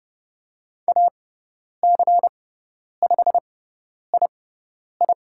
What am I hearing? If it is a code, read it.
Morse code, heard as a keyed tone, about 30 wpm, spaced as for 10 wpm, 700 Hz. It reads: AC5II
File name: Lesson 17g - Letter u - Call Signs 30wpm